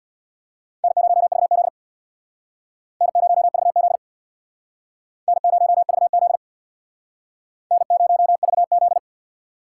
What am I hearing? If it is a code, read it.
Morse code, heard as a keyed tone, about 62 wpm, 700 Hz. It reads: N0VZ N0VZ N0VZ N0VZ
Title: Speed Racing - USA Call Signs 50wpm